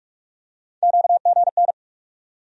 Morse code, heard as a keyed tone, about 41 wpm, 700 Hz. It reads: QGN
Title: USA Call Sign Postfixes - 3 Letters 45wpm